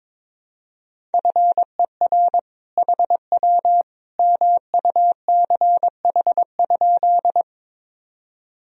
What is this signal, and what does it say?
Morse code, heard as a keyed tone, about 22 wpm, 700 Hz. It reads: FER HW MUCH?
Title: Sentences from Top 200 Words and Abbreviations 22wpm